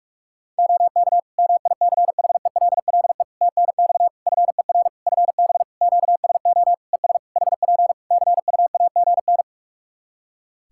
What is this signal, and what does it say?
Morse code, heard as a keyed tone, about 46 wpm, 700 Hz. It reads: OK MICHELLE TNX FER FB QSO ES HP CUAGN
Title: Practice QSO - Exchange 4 - Ending - 45wpm